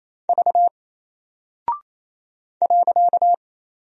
Morse code, heard as a keyed tone, about 27 wpm, 700 Hz. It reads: V .
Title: Single Character (Letter:Number:Punctuation:Prosign) Alt 28wpm